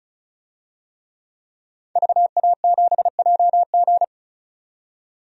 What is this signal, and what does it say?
Morse code, heard as a keyed tone, about 34 wpm, 700 Hz. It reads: VA7JG
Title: DXCC-100 Call Signs - 5 Characters - 35wpm